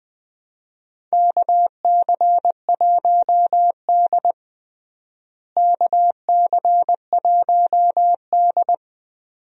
Morse code, heard as a keyed tone, about 20 wpm, 700 Hz. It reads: KC1D KC1D
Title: USA Call Signs - 4 Characters - Repeated 2x - 20wpm